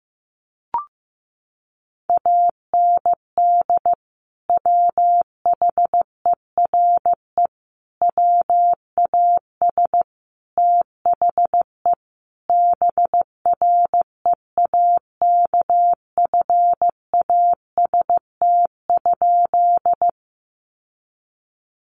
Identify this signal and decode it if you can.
Morse code, heard as a keyed tone, about 15 wpm, 700 Hz. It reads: AND WHERE WAS THE BREAKFAST?
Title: Sentences from Top 1500 Words 15wpm